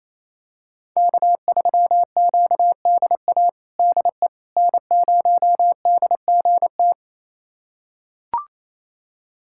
Morse code, heard as a keyed tone, about 27 wpm, 700 Hz. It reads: K3QDA DE N0DGT
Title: USA Callsign DE Callsign - 5 Characters - 28wpm